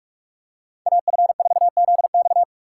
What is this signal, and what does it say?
Morse code, heard as a keyed tone, about 40 wpm, 700 Hz. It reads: AF4ZX